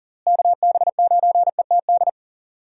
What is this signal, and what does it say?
Morse code, heard as a keyed tone, about 40 wpm, 700 Hz. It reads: KB9ETD
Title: USA Call Signs - 6 Characters - Repeated 3x - 40wpm